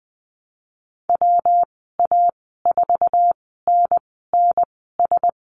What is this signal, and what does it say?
Morse code, heard as a keyed tone, about 19 wpm, spaced as for 14 wpm, 700 Hz. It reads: WA4NNS
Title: 122 Lesson 20g - Number 3 - Call Signs 20wpm